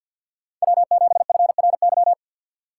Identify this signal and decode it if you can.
Morse code, heard as a keyed tone, about 61 wpm, 700 Hz. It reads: W7FRY
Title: Speed Racing - USA 1x3 Call Signs 50wpm